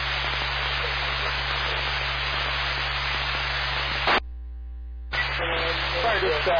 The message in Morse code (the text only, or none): none